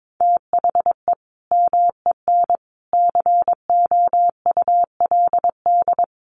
Morse code, heard as a keyed tone, about 22 wpm, 700 Hz. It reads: THE MEN COULD
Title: Sentences from Top 1200 Words 22wpm